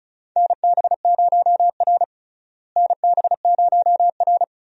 Morse code, heard as a keyed tone, about 36 wpm, 700 Hz. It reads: NB0R NB0R
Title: USA Call Signs - 4 Characters - Repeated 2x - 35wpm